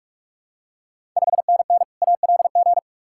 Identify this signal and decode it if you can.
Morse code, heard as a keyed tone, about 41 wpm, 700 Hz. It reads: 5NN ALG
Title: Ontario QSO Party Exchange 45wpm